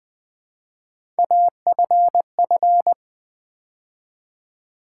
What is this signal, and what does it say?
Morse code, heard as a keyed tone, about 20 wpm, 700 Hz. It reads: AFF
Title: Build A Word - 3 Letters At A Time - 6-Letter Words - 20wpm